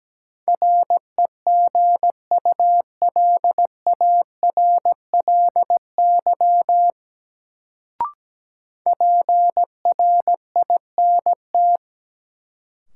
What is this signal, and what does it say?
Morse code, heard as a keyed tone, about 17 wpm, 700 Hz. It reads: REGULARLY PRINT
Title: Sentences from Top 1400 Words - New Word Drill - 17wpm